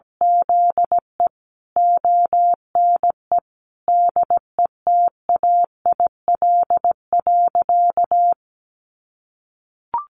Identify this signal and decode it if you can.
Morse code, heard as a keyed tone, about 17 wpm, 700 Hz. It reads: ZE ONE DETAIL.
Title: Sentences from Top 1500 Words 17wpm